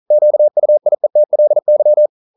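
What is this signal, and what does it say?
Morse code, heard as a keyed tone, about 42 wpm, 600 Hz. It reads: QUIETLY
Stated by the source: Morse code practice at 41 wpm, from CW med Gnister.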